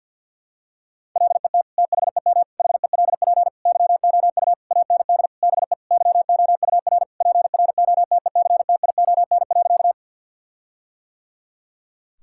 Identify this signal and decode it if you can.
Morse code, heard as a keyed tone, about 51 wpm, 700 Hz. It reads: LET THEM HELP YOU AND BE YOUR PROTECTION.
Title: Sentences from Top 1600 Words 50wpm